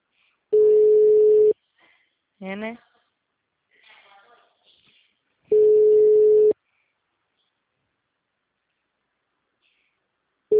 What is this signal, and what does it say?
Ringback tone (Europe, 425 Hz), with speech.